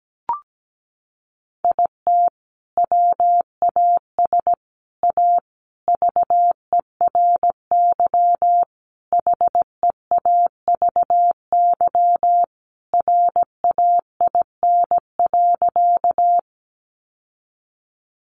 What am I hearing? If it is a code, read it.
Morse code, heard as a keyed tone, about 17 wpm, 700 Hz. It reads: IT WAS A VERY HEAVY RAIN.
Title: Sentences from Top 1100 Words 17wpm